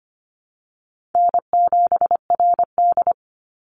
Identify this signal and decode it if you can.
Morse code, heard as a keyed tone, about 25 wpm, 700 Hz. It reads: N7RD
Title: Call Signs - 25wpm